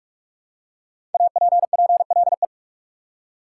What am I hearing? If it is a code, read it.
Morse code, heard as a keyed tone, about 46 wpm, 700 Hz. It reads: APPLE
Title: Five Letter Words - 45wpm